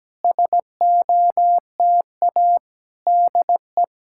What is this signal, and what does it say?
Morse code, heard as a keyed tone, about 17 wpm, 700 Hz. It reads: SOTA DE
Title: SOTA QSO Practice 17wpm